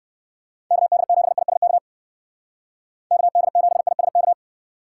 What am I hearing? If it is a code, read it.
Morse code, heard as a keyed tone, about 54 wpm, 700 Hz. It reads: KD7ISK KD7ISK
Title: Speed Racing - USA Call Signs 45wpm